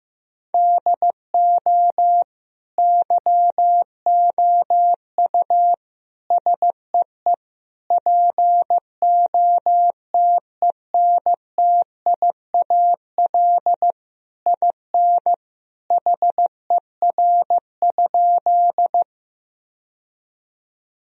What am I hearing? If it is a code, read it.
Morse code, heard as a keyed tone, about 16 wpm, 700 Hz. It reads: DO YOU SEE POTENTIAL IN HER?
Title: Sentences from Top 1000 Words - Review Part 4 - 15wpm